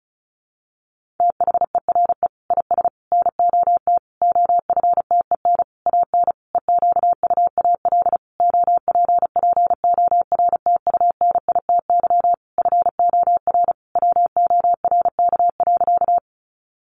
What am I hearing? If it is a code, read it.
Morse code, heard as a keyed tone, about 34 wpm, 700 Hz. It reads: THERE IS NOT OFTEN AN EQUAL OPPORTUNITY FOR WORK.
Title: Sentences from Top 1400 Words 35wpm